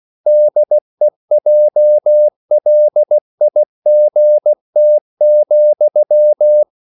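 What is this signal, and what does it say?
Morse code, heard as a keyed tone, about 16 wpm, 600 Hz. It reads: DEJLIGT,
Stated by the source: Slow (QRS) Morse code at 16 wpm, from CW med Gnister.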